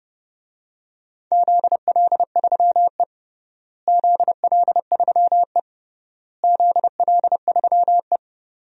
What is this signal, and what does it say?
Morse code, heard as a keyed tone, about 28 wpm, 700 Hz. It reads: ZL3E ZL3E ZL3E
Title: DXCC-100 Call Signs - 4 Characters - Repeated 3x - 30wpm